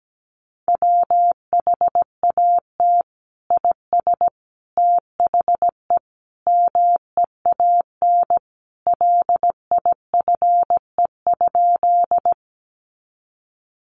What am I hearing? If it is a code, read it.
Morse code, heard as a keyed tone, about 17 wpm, 700 Hz. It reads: WHAT IS THE MEAN LIFE?
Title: Sentences from Top 900 Words 17wpm